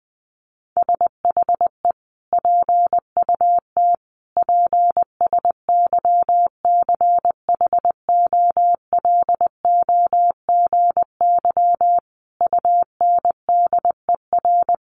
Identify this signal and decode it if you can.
Morse code, heard as a keyed tone, about 19 wpm, 700 Hz. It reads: SHE PUT PSYCHOLOGY UNDER
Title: Sentences from Top 1400 Words 20wpm